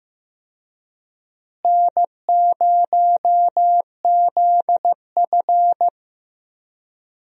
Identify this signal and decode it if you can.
Morse code, heard as a keyed tone, about 16 wpm, 700 Hz. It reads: N0ZF
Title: USA Call Signs - 4 Characters - Repeated 3x - 15wpm